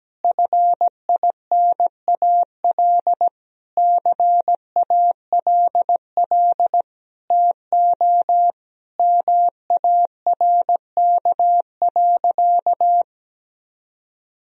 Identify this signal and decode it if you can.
Morse code, heard as a keyed tone, about 17 wpm, 700 Hz. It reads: FINAL CALL TO MARK.
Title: Sentences from Top 1000 Words - Review Part 2 - 17wpm